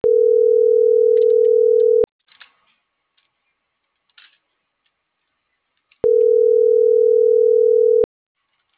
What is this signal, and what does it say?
Ringback tone (North America), 440 + 480 Hz.